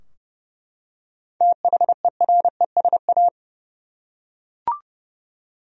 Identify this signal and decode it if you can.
Morse code, heard as a keyed tone, about 30 wpm, 700 Hz. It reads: THERESA
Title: USA Common Names - 30wpm